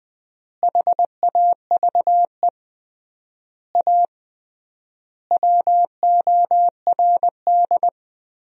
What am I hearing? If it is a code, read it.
Morse code, heard as a keyed tone, about 20 wpm, 700 Hz. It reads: HAVE A WORD
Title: 3x Word-Spacing - Sentences from Top 300 Words - 20wpm